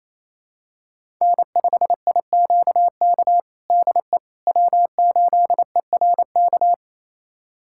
Morse code, heard as a keyed tone, about 28 wpm, 700 Hz. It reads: N5IQK DE W8ERK
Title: USA Callsign DE Callsign - 5 Characters - 28wpm